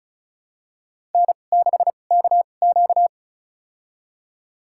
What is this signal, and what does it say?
Morse code, heard as a keyed tone, about 34 wpm, spaced as for 25 wpm, 700 Hz. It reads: N6KQ
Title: USA Call Signs - 2x Character Spacing - 35wpm